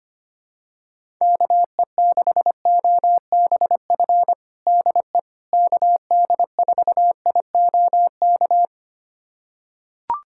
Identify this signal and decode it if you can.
Morse code, heard as a keyed tone, about 25 wpm, 700 Hz. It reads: KE6OBF DE KD4IOK
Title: USA Callsign DE Callsign 25wpm